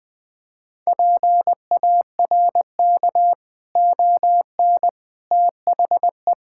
Morse code, heard as a keyed tone, about 20 wpm, 700 Hz. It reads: PARK ON THE
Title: Sentences with 6-Word Phrases - 20wpm